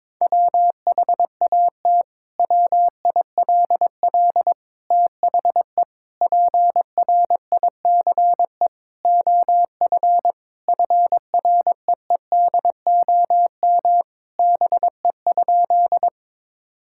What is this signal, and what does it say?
Morse code, heard as a keyed tone, about 22 wpm, 700 Hz. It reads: WHAT WILL THE PRICE OF FREEDOM BE?
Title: Sentences from Top 1000 Words - Review Part 5 - 22wpm